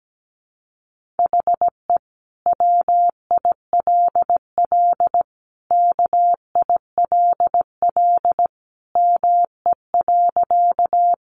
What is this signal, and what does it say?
Morse code, heard as a keyed tone, about 17 wpm, 700 Hz. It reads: HE WILL KILL ME.
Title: Sentences from Top 1400 Words 17wpm